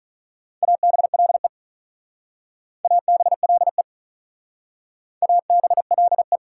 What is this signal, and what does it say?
Morse code, heard as a keyed tone, about 45 wpm, 700 Hz. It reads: ABLE ABLE ABLE